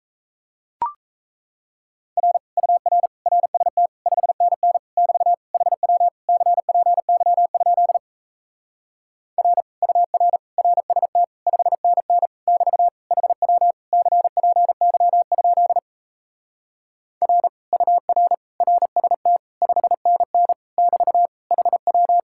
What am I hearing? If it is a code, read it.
Morse code, heard as a keyed tone, about 37 wpm, 700 Hz. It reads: R UR RST 5NN <BT> HW CPY? R UR RST 5NN <BT> HW CPY? R UR RST 5NN <BT> HW